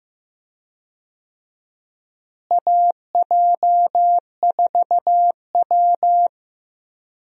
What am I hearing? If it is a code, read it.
Morse code, heard as a keyed tone, about 15 wpm, 700 Hz. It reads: AJ4W
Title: USA Call Signs - 4 Characters - Repeated 2x - 15wpm